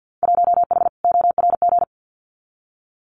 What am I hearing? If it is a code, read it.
Morse code, heard as a keyed tone, about 50 wpm, 700 Hz. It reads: 1H ORG
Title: Winter Field Day Exchanges 50wpm